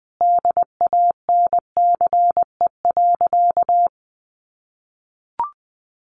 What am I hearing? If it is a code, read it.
Morse code, heard as a keyed tone, about 19 wpm, 700 Hz. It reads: DANCE.
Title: Sentences from Top 900 Words 20wpm